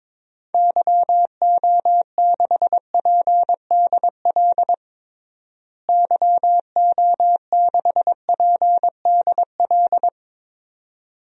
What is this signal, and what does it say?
Morse code, heard as a keyed tone, about 22 wpm, 700 Hz. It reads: YO6PDL YO6PDL
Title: DXCC-100 Call Signs - 6 Characters - Repeated 2x - 22wpm